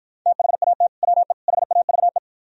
Morse code, heard as a keyed tone, about 52 wpm, 700 Hz. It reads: THAT WE HAVE